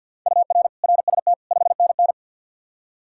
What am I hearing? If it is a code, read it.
Morse code, heard as a keyed tone, about 49 wpm, 700 Hz. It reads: UR RST 5NN
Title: K1USN SST QSO Practice 50wpm